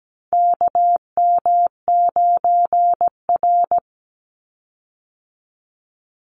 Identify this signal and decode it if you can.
Morse code, heard as a keyed tone, about 17 wpm, 700 Hz. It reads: KM9R
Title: Call Signs - 17wpm